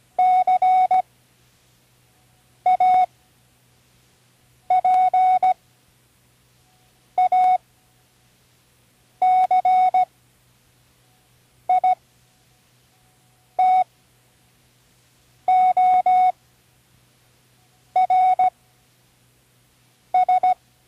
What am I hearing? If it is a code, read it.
Morse code, heard as a keyed tone, about 17 wpm, spaced as for 5 wpm, 740 Hz. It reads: CAPACITORS
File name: Learning Morse Code (from Tune in the World)